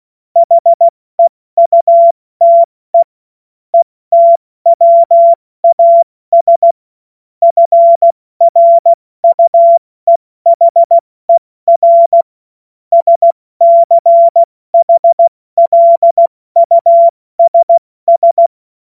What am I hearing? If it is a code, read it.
Morse code, heard as a keyed tone, about 16 wpm, 680 Hz. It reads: HEUTE ETWAS FRUEHER SCHLUSS
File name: Morsekurs / Witze / Witz 04. April 70 F80